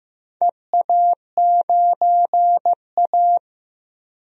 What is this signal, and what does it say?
Morse code, heard as a keyed tone, about 15 wpm, 700 Hz. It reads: EA9A
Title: DXCC-100 Call Signs - 4 Characters - Repeated 2x - 15wpm